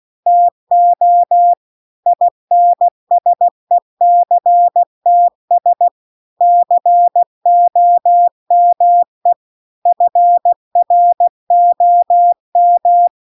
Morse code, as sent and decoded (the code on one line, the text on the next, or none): - --- / .. -. ... -.-. - ... / -.-. --- -- . / ..-. .-. --- --
TO INSECTS COME FROM